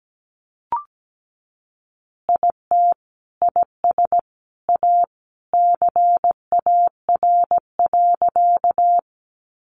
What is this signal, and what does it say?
Morse code, heard as a keyed tone, about 17 wpm, 700 Hz. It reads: IT IS A CAR.
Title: Easy Phrases 17wpm